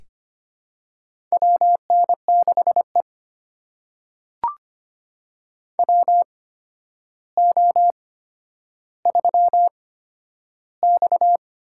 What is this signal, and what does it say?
Morse code, heard as a keyed tone, about 25 wpm, spaced as for 11 wpm, 700 Hz. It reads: WN6E WO3X
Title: USA Call Signs - 8x Character Spacing - 25wpm